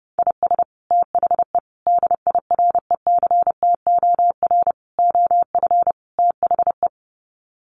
Morse code, heard as a keyed tone, about 30 wpm, 700 Hz. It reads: IS THE DIRECTOR OF THE
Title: Sets of 5 Words Encore2 30wpm